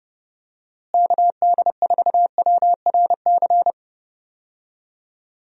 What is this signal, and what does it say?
Morse code, heard as a keyed tone, about 30 wpm, 700 Hz. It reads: KD4WRC
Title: USA Call Signs - 6 Characters - Repeated 3x - 30wpm